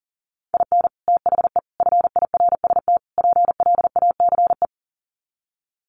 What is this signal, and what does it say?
Morse code, heard as a keyed tone, about 40 wpm, 700 Hz. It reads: IN THE FIRST PLACE
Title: Sets of 4 Words 40wpm